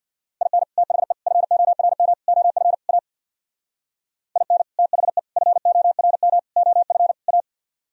Morse code, heard as a keyed tone, about 62 wpm, 700 Hz. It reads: IN THE FORM OF A IN THE FORM OF A
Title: Speed-Racing - Sets of 5 Words 50wpm